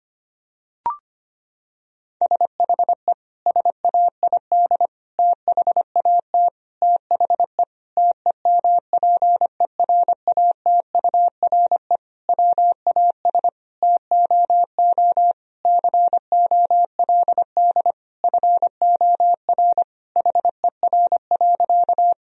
Morse code, heard as a keyed tone, about 25 wpm, 700 Hz. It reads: SHE SAID THAT THE TEMPERATURE WAS TOO COLD FOR HER.
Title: Sentences from Top 1000 Words - Review Part 4 - 25wpm